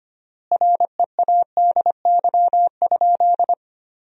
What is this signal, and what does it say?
Morse code, heard as a keyed tone, about 25 wpm, 700 Hz. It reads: READY?
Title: 4x Word-Spacing - Sentences from Top 400 Words - 25wpm